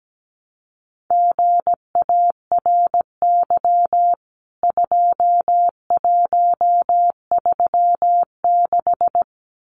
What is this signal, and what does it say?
Morse code, heard as a keyed tone, about 17 wpm, 700 Hz. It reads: GARY 2136
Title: CWT Exchange 17wpm